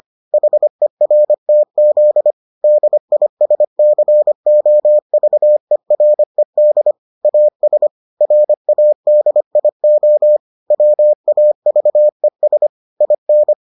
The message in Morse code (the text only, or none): HERTZ DISCOVERED AS RADIO WAVES IN